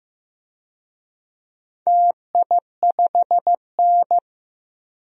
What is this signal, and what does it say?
Morse code, heard as a keyed tone, about 15 wpm, 700 Hz. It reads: TI5N
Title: DXCC-100 Call Signs - 4 Characters - Repeated 3x - 15wpm